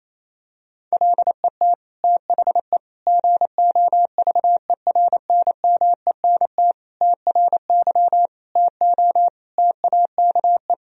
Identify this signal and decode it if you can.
Morse code, heard as a keyed tone, about 28 wpm, 700 Hz. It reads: LET THE GOVERNMENT TRY TO TAKE
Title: Sentences from Top 1000 Words - Review Part 3 - 28wpm